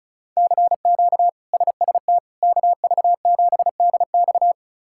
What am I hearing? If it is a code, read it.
Morse code, heard as a keyed tone, about 36 wpm, 700 Hz. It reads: CQ SST KV7DX
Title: K1USN SST QSO Practice - 35wpm